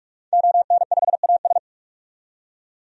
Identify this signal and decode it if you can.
Morse code, heard as a keyed tone, about 41 wpm, 700 Hz. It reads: ON5AS
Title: DXCC-100 Call Signs - 5 Characters - 45wpm